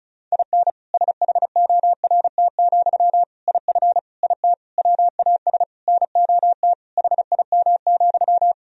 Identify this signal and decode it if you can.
Morse code, heard as a keyed tone, about 34 wpm, 700 Hz. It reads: IN SHORT, IF IT WAS NOT HIM,
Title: Sentences from Top 1000 Words - Review Part 2 - 35wpm